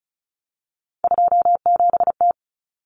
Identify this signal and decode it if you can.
Morse code, heard as a keyed tone, about 34 wpm, 700 Hz. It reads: 27T
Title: Common Cut Numbers - 3-Digit 35wpm